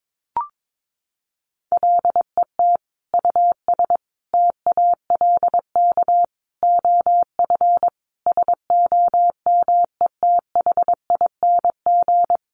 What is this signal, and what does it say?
Morse code, heard as a keyed tone, about 22 wpm, 700 Hz. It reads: LET US TALK OF SOMETHING
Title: Sentences from Top 800 Words 22wpm